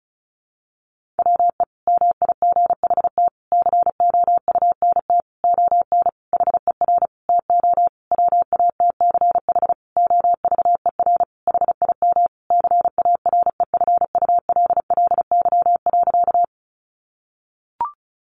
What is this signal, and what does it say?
Morse code, heard as a keyed tone, about 34 wpm, 700 Hz. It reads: WE MIGHT COUNT ON HER TO WATCH OVER HIM CAREFULLY.
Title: Sentences from Top 1500 Words 35wpm